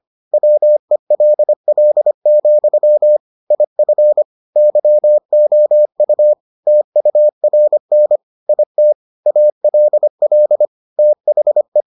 Morse code, heard as a keyed tone, about 25 wpm, 600 Hz. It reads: WELL, IF YOU TURN IT ALL THE